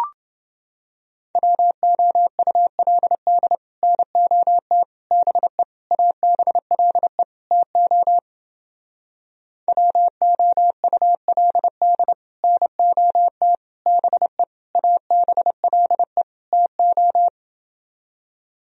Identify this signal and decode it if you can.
Morse code, heard as a keyed tone, about 29 wpm, 700 Hz. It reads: WOULD NOT BE ABLE TO WOULD NOT BE ABLE TO